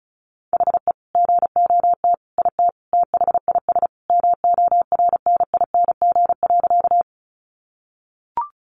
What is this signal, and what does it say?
Morse code, heard as a keyed tone, about 34 wpm, 700 Hz. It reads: HE GOT IT THIS MORNING.
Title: Sentences from Top 400 Words 35wpm